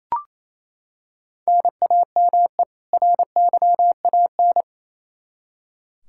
Morse code, heard as a keyed tone, about 28 wpm, 700 Hz. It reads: NAME RYAN
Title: K1USN SST QSO Practice 28wpm